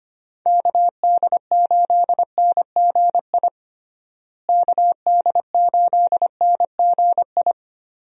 Morse code, heard as a keyed tone, about 25 wpm, 700 Hz. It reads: KD8NGI KD8NGI
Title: USA Call Signs - 6 Characters - Repeated 2x - 25wpm